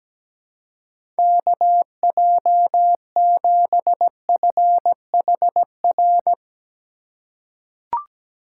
Morse code, heard as a keyed tone, about 17 wpm, 700 Hz. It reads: KJ7FHR
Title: USA Call Signs - 6 Characters - 2x Character Spacing - 17wpm